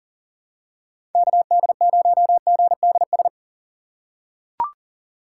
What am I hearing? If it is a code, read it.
Morse code, heard as a keyed tone, about 40 wpm, 700 Hz. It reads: KD0GDS
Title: USA Call Signs - 2x Character Spacing - 40wpm